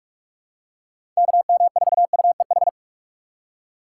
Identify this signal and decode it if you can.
Morse code, heard as a keyed tone, about 46 wpm, 700 Hz. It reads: KM4UEH
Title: USA Call Signs - 6 Characters - 3x Character Spacing - 45wpm